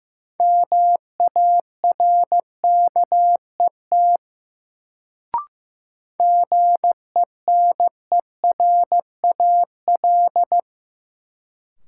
Morse code, heard as a keyed tone, about 15 wpm, 700 Hz. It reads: MARKET GENERAL
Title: Sentences from Top 600 Words - New Word Drill - 15wpm